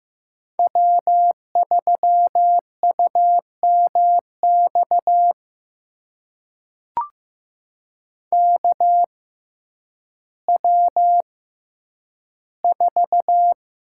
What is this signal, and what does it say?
Morse code, heard as a keyed tone, about 15 wpm, spaced as for 9 wpm, 700 Hz. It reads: W3UMX KW4
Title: USA Call Signs - 5 Characters - 6x Character Spacing - 15wpm